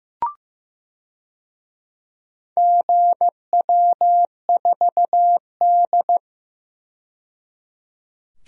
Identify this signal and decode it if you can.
Morse code, heard as a keyed tone, about 16 wpm, 700 Hz. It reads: GW4D
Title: DXCC-100 Call Signs - 4 Characters - 15wpm